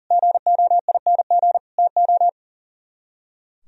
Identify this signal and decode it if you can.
Morse code, heard as a keyed tone, about 51 wpm, 700 Hz. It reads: GOING TO